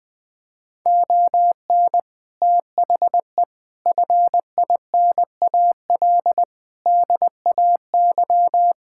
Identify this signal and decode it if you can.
Morse code, heard as a keyed tone, about 20 wpm, 700 Hz. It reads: ON THE FINAL DAY
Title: Sentences with 6-Word Phrases - 20wpm